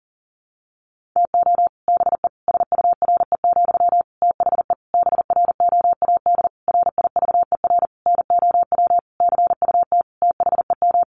Morse code, heard as a keyed tone, about 40 wpm, 700 Hz. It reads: TO BE SURE, THE BROAD RIVER NOW CUT THEM
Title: Sentences from Top 1000 Words 40wpm